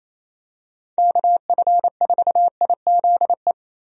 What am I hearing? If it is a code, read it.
Morse code, heard as a keyed tone, about 28 wpm, 700 Hz. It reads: KF4IZE
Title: USA Call Signs - 6 Characters - Repeated 2x - 28wpm